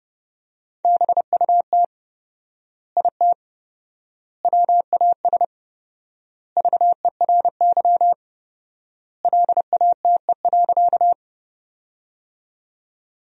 Morse code, heard as a keyed tone, about 30 wpm, 700 Hz. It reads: BUT IT WAS VERY LATE.
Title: 4x Word-Spacing - Sentences from Top 300 Words - 30wpm